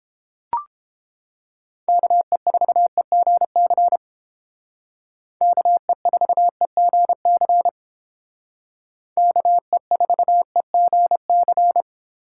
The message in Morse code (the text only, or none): KE4EGC KE4EGC KE4EGC